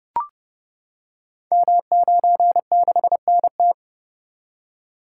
Morse code, heard as a keyed tone, about 29 wpm, 700 Hz. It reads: M96NT
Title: Code Groups - 5 Letter:Numbers - 30wpm